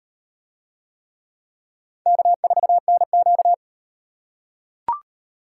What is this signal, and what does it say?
Morse code, heard as a keyed tone, about 34 wpm, 700 Hz. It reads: K4NQ